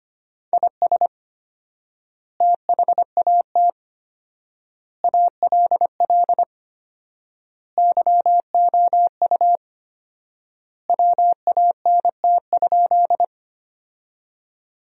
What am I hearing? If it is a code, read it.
Morse code, heard as a keyed tone, about 25 wpm, 700 Hz. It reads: IS THAT ALL YOU WANT?
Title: 4x Word-Spacing - Sentences from Top 200 Words - 25wpm